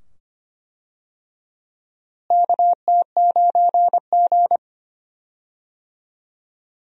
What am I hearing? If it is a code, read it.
Morse code, heard as a keyed tone, about 25 wpm, 700 Hz. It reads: KT9G
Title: USA Call Signs - 4 Characters - Repeated 2x - 25wpm